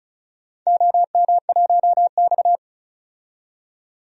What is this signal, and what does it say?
Morse code, heard as a keyed tone, about 36 wpm, 700 Hz. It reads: OM1X
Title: DXCC-100 Call Signs - 4 Characters - 35wpm